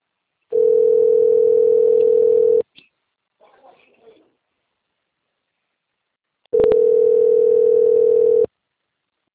Ringback tone (North America), 440 + 480 Hz.